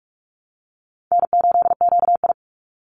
Morse code, heard as a keyed tone, about 44 wpm, 700 Hz. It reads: N8QI
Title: USA Call Signs 45wpm